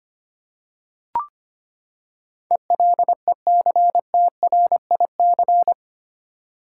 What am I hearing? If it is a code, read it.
Morse code, heard as a keyed tone, about 25 wpm, 700 Hz. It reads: ELECTRIC
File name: Eight Letter Words - 25wpm